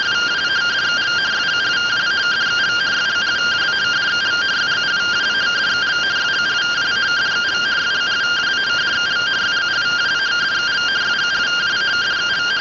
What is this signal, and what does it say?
Two-tone teleprinter signal, about 170 Hz shift.